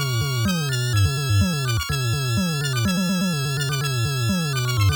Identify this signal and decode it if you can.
There is no code.